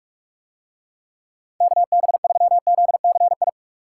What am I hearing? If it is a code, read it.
Morse code, heard as a keyed tone, about 44 wpm, 700 Hz. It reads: KB3ZCI